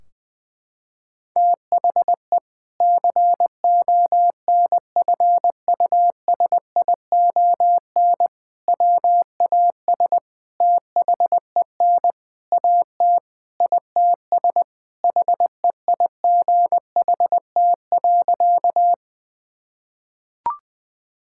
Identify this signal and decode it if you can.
Morse code, heard as a keyed tone, about 20 wpm, 700 Hz. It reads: THE CONFUSION WAS THEN AT ITS HEIGHT.
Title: Sentences from Top 1600 Words 20wpm